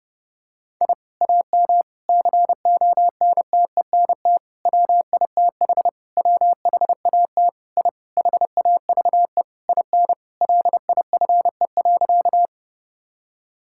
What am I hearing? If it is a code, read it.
Morse code, heard as a keyed tone, about 32 wpm, 700 Hz. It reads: I AM CONTENT WITH WHAT I HAVE IN LIFE.
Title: Sentences from Top 1000 Words - Review Part 6 - 30wpm